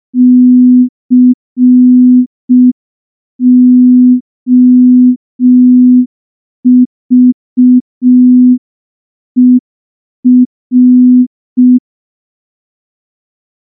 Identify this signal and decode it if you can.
Morse code, heard as a keyed tone, about 5 wpm, 250 Hz. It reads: COVER
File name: morse / morse-training-05WPM